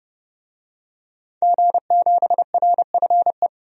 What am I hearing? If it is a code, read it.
Morse code, heard as a keyed tone, about 30 wpm, 700 Hz. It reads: G7RFE
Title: DXCC-100 Call Signs - 5 Characters - 30wpm